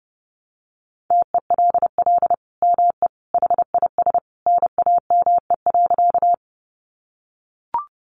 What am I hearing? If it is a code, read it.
Morse code, heard as a keyed tone, about 30 wpm, 700 Hz. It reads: TELL ME HIS NAME.
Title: Sentences from Top 200 Words 30wpm